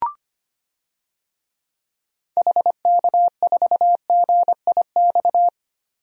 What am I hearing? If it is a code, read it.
Morse code, heard as a keyed tone, about 25 wpm, 700 Hz. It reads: HK4GIX
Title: DXCC-100 Call Signs - 6 Characters - 25wpm